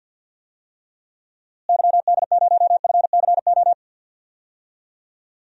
Morse code, heard as a keyed tone, about 50 wpm, 700 Hz. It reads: YD0FCO
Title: DXCC-100 Call Signs - 6 Characters - Repeated 2x - 50wpm